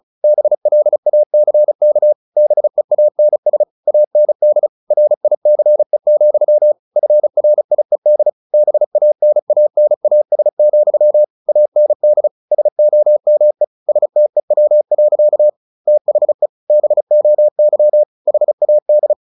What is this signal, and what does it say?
Morse code, heard as a keyed tone, about 34 wpm, 600 Hz. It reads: BLACK BEANS AND RICE, FRIED BANANAS, AND SOME STEW. THE BOY HAD
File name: the old man and the sea / the old man and the sea 1 35wpm